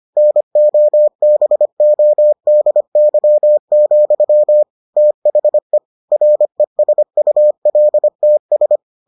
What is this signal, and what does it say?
Morse code, heard as a keyed tone, about 25 wpm, 600 Hz. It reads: NOBODY, THE RESULTS